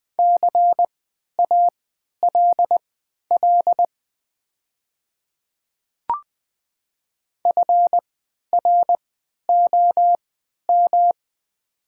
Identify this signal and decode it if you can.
Morse code, heard as a keyed tone, about 20 wpm, spaced as for 11 wpm, 700 Hz. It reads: CALL FROM
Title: 3x Character Spacing - Top 200 Words - 20wpm